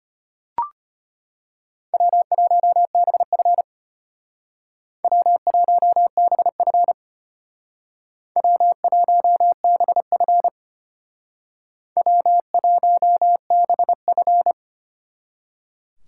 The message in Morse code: W1BF W1BF W1BF W1BF

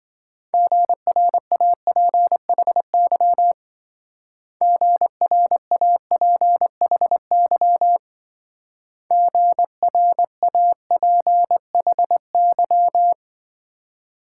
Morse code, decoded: GRAPHY GRAPHY GRAPHY

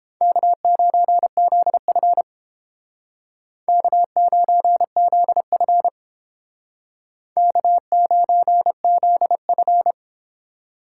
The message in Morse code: K9ZF K9ZF K9ZF

